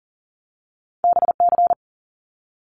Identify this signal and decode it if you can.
Morse code, heard as a keyed tone, about 41 wpm, 700 Hz. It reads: BC
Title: Canadian Provinces and Territories 40wpm